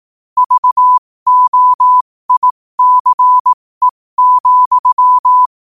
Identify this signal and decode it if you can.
Morse code, heard as a keyed tone, about 18 wpm, spaced as for 15 wpm, 1000 Hz. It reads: VOICE,
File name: qotdfiles / 15wpm